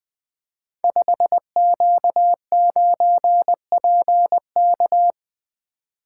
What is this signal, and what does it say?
Morse code, heard as a keyed tone, about 20 wpm, 700 Hz. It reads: 5Q9PK
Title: Code Groups - 5 Letter:Numbers - 20wpm